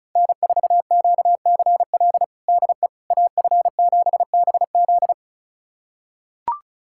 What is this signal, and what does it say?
Morse code, heard as a keyed tone, about 34 wpm, 700 Hz. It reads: N4QCL DE AF7BZ
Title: USA Callsign DE Callsign - 5 Characters - 35wpm